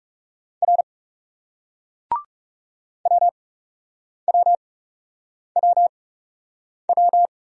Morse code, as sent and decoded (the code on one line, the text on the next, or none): .-. / .-- .-- .-- .--
R WWWW